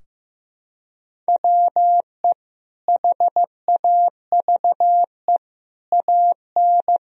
Morse code, heard as a keyed tone, about 15 wpm, 700 Hz. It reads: WE HAVE AN
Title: Sentences from Top 1000 Words - Review Part 3 - 15wpm